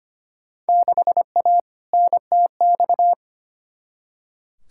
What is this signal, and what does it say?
Morse code, heard as a keyed tone, about 25 wpm, 700 Hz. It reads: 6A NTX
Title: Field Day Exchanges - 25wpm